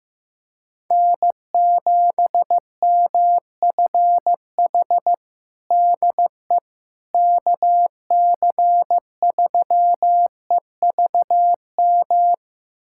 Morse code, heard as a keyed tone, about 15 wpm, 700 Hz. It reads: N7MFH DE KC3EVM
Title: USA Callsign DE Callsign 15wpm